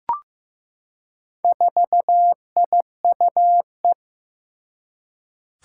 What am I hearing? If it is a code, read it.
Morse code, heard as a keyed tone, about 15 wpm, 700 Hz. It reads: 4IUE
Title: Four Challenging Letter-Number Combinations 15wpm